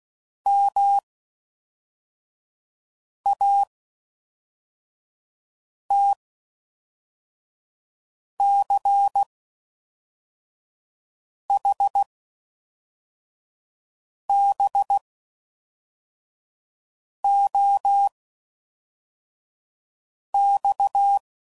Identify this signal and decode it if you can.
Morse code, heard as a keyed tone, about 16 wpm, spaced as for 4 wpm, 800 Hz. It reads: MATCHBOX